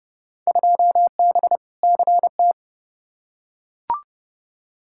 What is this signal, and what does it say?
Morse code, heard as a keyed tone, about 31 wpm, 700 Hz. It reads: 2B CT
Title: Field Day Exchanges Encore - 30wpm